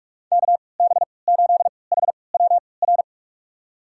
Morse code, heard as a keyed tone, about 46 wpm, spaced as for 26 wpm, 700 Hz. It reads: KB8HWR
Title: USA Call Signs - 6 Characters - 3x Character Spacing - 45wpm